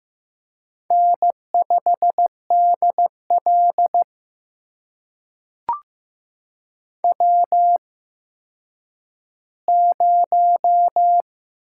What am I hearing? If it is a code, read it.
Morse code, heard as a keyed tone, about 15 wpm, spaced as for 9 wpm, 700 Hz. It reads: N5DL W0
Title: USA Call Signs - 4 Characters - 8x Character Spacing - 15wpm